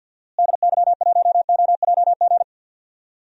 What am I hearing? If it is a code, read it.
Morse code, heard as a keyed tone, about 52 wpm, 700 Hz. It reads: DY1OJG
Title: DXCC-100 Call Signs - 6 Characters - Repeated 2x - 50wpm